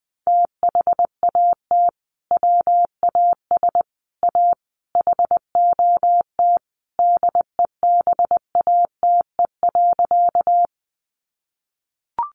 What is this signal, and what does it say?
Morse code, heard as a keyed tone, about 20 wpm, 700 Hz. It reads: THAT WAS A HOT DEBATE.
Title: Sentences from Top 1100 Words 20wpm